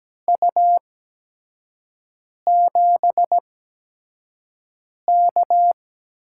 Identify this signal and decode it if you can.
Morse code, heard as a keyed tone, about 17 wpm, spaced as for 5 wpm, 700 Hz. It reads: U7K
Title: USA Call Signs - 8x Character Spacing - 17wpm